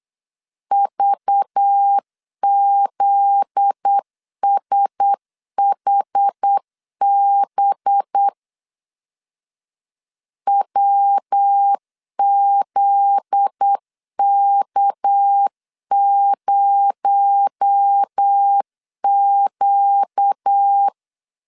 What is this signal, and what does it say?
Morse code, heard as a keyed tone, about 8 wpm, 800 Hz. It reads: VZSHB WZK0Q
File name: LEZIONI / 7 GF9Z / 7 GF9Z 8wpm 3213